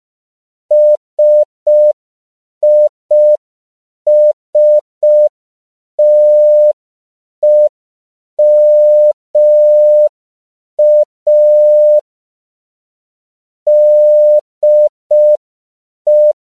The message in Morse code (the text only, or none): SISTEMA DE